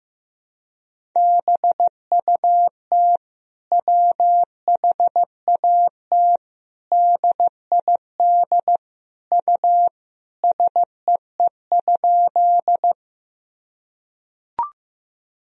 Morse code, heard as a keyed tone, about 16 wpm, 700 Hz. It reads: BUT WHAT DID U SEE?
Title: Sentences from Top 100 Words and Abbreviations 15wpm